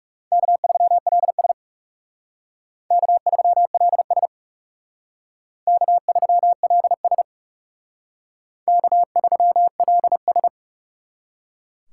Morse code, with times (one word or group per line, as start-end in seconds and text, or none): 0.3-1.5 K3LS
2.9-4.3 K3LS
5.7-7.2 K3LS
8.7-10.5 K3LS